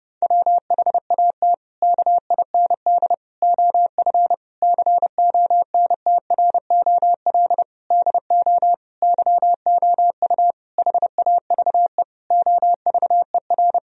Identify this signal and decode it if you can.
Morse code, heard as a keyed tone, about 31 wpm, 700 Hz. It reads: WHAT KIND OF CONTROL DO YOU HAVE OVER
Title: Sentences from Top 1000 Words - Review Part 3 - 30wpm